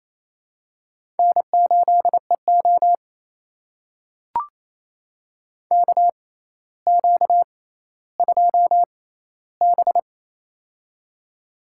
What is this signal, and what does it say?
Morse code, heard as a keyed tone, about 27 wpm, spaced as for 14 wpm, 700 Hz. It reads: N8EO KQ2B
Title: USA Call Signs - 4 Characters - 6x Character Spacing - 28wpm